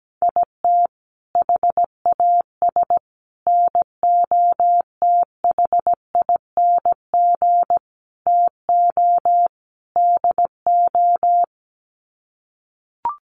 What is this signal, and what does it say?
Morse code, heard as a keyed tone, about 17 wpm, 690 Hz. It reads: IT HAS NOTHING TO DO
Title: Sets of 5 Words Encore 17wpm